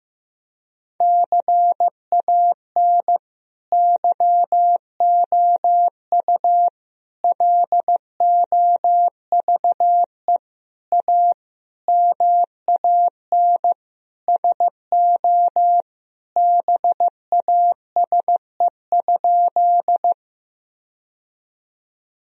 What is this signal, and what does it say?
Morse code, heard as a keyed tone, about 15 wpm, 700 Hz. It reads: CAN YOU LOVE A MAN SO BASE?
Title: Sentences from Top 1000 Words - Review Part 1 - 15wpm